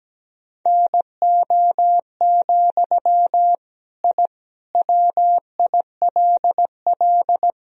Morse code, heard as a keyed tone, about 17 wpm, 700 Hz. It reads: NO, I WILL
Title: Sentences from Top 1000 Words - Review Part 1 - 17wpm